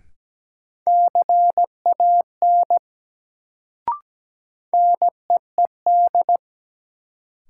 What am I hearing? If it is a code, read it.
Morse code, heard as a keyed tone, about 17 wpm, 700 Hz. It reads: CAN NEED
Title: Sets of 2 Words - New Word Drill - 17wpm